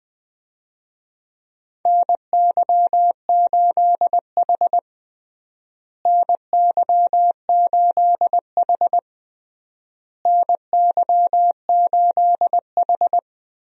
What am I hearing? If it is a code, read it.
Morse code, heard as a keyed tone, about 21 wpm, 700 Hz. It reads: NY8H NY8H NY8H
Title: USA Call Signs - 4 Characters - Repeated 3x - 20wpm